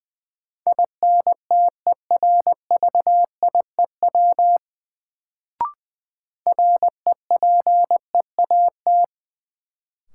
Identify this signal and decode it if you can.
Morse code, heard as a keyed tone, about 20 wpm, 700 Hz. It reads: INTERVIEW REPEAT
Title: Sentences from Top 1600 Words - New Word Drill - 20wpm